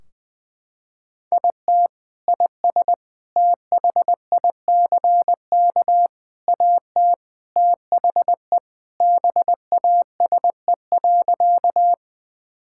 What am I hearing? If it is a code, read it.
Morse code, heard as a keyed tone, about 20 wpm, 700 Hz. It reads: IT IS THICK AT THE BASE.
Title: Sentences from Top 1600 Words 20wpm